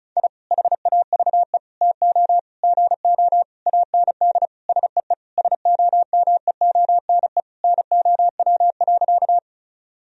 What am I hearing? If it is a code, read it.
Morse code, heard as a keyed tone, about 36 wpm, 700 Hz. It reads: I HAVE TO GO AND SEE SOMEONE NOW.
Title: Sentences from Top 1000 Words - Review Part 4 - 35wpm